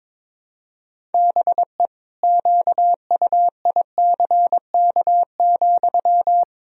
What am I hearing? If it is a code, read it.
Morse code, heard as a keyed tone, about 22 wpm, 700 Hz. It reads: BE QUICK,
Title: Sentences from Top 500 Words and Abbreviations 22wpm